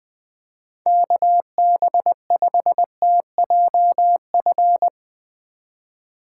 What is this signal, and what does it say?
Morse code, heard as a keyed tone, about 20 wpm, 700 Hz. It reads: KB5TJF
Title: Speed Racing - USA 2x3 Call Signs 17wpm